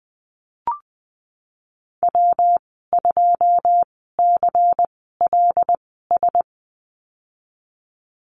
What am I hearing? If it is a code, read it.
Morse code, heard as a keyed tone, about 20 wpm, spaced as for 14 wpm, 700 Hz. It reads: W2CLS
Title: 147 Lesson 24g - Letter y - Call Signs 20wpm